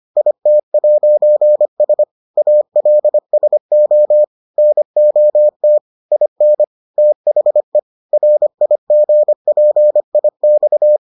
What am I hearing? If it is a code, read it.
Morse code, heard as a keyed tone, about 25 wpm, 600 Hz. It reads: IT'S ALSO NOT IN THE RIGPIX